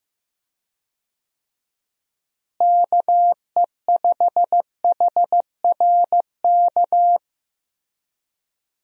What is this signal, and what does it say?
Morse code, heard as a keyed tone, about 15 wpm, 700 Hz. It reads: KE5HRK
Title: USA Call Signs - 6 Characters - Repeated 2x - 15wpm